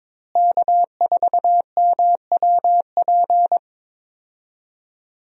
Morse code, heard as a keyed tone, about 22 wpm, 700 Hz. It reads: K4MWP
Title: USA Call Signs - 5 Characters - Repeated 3x - 22wpm